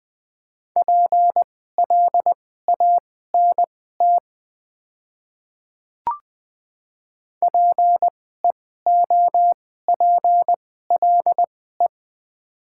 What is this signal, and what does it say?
Morse code, heard as a keyed tone, about 20 wpm, spaced as for 15 wpm, 700 Hz. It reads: PLANT PEOPLE
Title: 2x Character Spacing - Top 300 Words - 20wpm